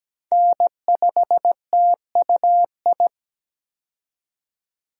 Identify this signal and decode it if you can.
Morse code, heard as a keyed tone, about 17 wpm, 700 Hz. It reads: N5TUI
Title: USA Call Signs - 5 Characters - 17wpm